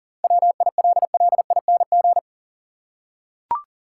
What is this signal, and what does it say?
Morse code, heard as a keyed tone, about 40 wpm, 700 Hz. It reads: WILLING
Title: Seven Letter Words - 40wpm